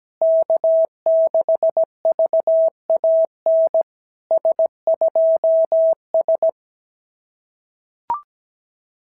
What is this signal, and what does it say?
Morse code, heard as a keyed tone, about 17 wpm, 650 Hz. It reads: K6VAN S2S
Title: SOTA QSO Practice 17wpm